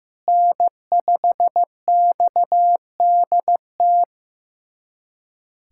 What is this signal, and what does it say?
Morse code, heard as a keyed tone, about 15 wpm, 700 Hz. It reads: N5XDT
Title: USA Callsign DE Callsign - 5 Characters - 15wpm